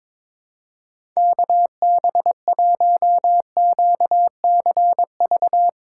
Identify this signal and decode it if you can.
Morse code, heard as a keyed tone, about 22 wpm, 700 Hz. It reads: KB1QCV
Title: USA Callsign DE Callsign - 6 Characters - 22wpm